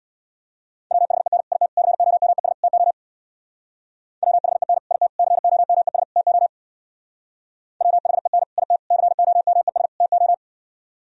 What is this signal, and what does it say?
Morse code, heard as a keyed tone, about 63 wpm, 700 Hz. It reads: WHEN IT COMES TO WHEN IT COMES TO WHEN IT COMES TO